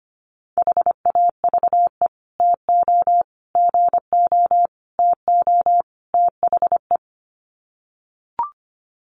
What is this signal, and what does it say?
Morse code, heard as a keyed tone, about 25 wpm, 700 Hz. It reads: HAVE TO GO TO THE
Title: Sets of 5 Words Encore 25wpm